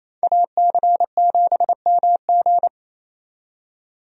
Morse code, heard as a keyed tone, about 28 wpm, 700 Hz. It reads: AC7MG
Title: USA Call Signs - 5 Characters - 28wpm